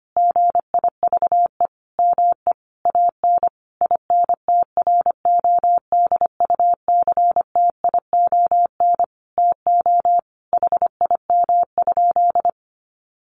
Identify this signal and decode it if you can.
Morse code, heard as a keyed tone, about 25 wpm, 700 Hz. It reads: GIVE ME AN INTRODUCTION TO HIM?
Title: Sentences from Top 1000 Words 25wpm